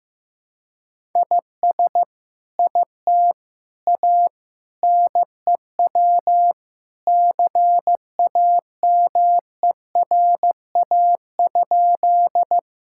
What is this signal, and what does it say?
Morse code, heard as a keyed tone, about 15 wpm, 700 Hz. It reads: IS IT A NEW CAMERA?
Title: Sentences from Top 1000 Words - Review Part 5 - 15wpm